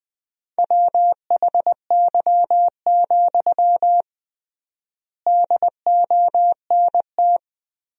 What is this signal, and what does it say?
Morse code, heard as a keyed tone, about 20 wpm, 700 Hz. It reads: WHY, DONT
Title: 3x Word-Spacing - Sentences from Top 300 Words - 20wpm